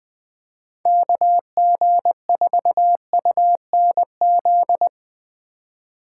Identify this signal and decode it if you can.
Morse code, heard as a keyed tone, about 21 wpm, 700 Hz. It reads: KG4UNZ
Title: Speed Racing - USA Call Signs 17wpm